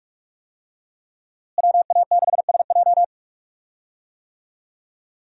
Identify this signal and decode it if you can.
Morse code, heard as a keyed tone, about 45 wpm, 700 Hz. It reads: WA6SJ
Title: USA Call Signs - 5 Characters - Repeated 2x - 45wpm